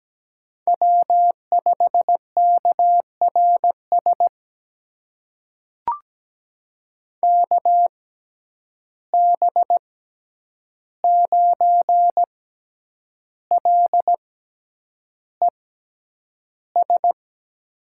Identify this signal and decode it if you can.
Morse code, heard as a keyed tone, about 17 wpm, spaced as for 8 wpm, 700 Hz. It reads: W5KRS KB9LES
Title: USA Call Signs - 6x Character Spacing - 17wpm